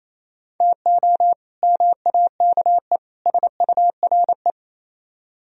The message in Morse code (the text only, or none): TO MAKE SURE